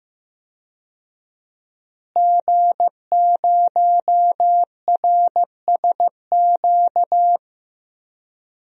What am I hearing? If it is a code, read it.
Morse code, heard as a keyed tone, about 16 wpm, 700 Hz. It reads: G0RSQ
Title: DXCC-100 Call Signs - 5 Characters - Repeated 3x - 15wpm